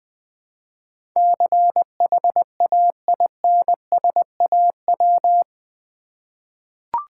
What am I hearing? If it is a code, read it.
Morse code, heard as a keyed tone, about 19 wpm, 700 Hz. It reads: CHAINSAW
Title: Difficult Words with Hints - 20wpm